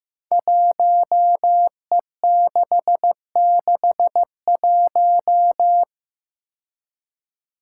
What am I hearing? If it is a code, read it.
Morse code, heard as a keyed tone, about 15 wpm, 700 Hz. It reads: 1E661
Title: Five Challenging Letter-Number Combinations 15wpm